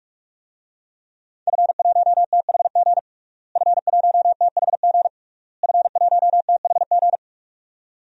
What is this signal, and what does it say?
Morse code, heard as a keyed tone, about 46 wpm, 700 Hz. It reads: F1THG F1THG F1THG
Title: DXCC-100 Call Signs - 5 Characters - Repeated 3x - 45wpm